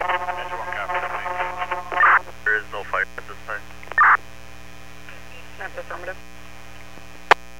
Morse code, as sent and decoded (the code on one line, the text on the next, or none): none